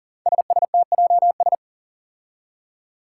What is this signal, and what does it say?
Morse code, heard as a keyed tone, about 40 wpm, 700 Hz. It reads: SSTJS
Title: Five Challenging Letter-Number Combinations 40wpm